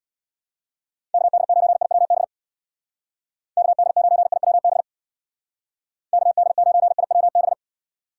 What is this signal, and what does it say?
Morse code, heard as a keyed tone, about 60 wpm, 700 Hz. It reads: KD9IWB KD9IWB KD9IWB
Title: Speed Racing - USA Call Signs 50wpm